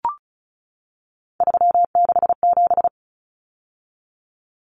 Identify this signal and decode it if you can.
Morse code, heard as a keyed tone, about 34 wpm, 700 Hz. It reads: 367
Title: Common Cut Numbers - 3-Digit 35wpm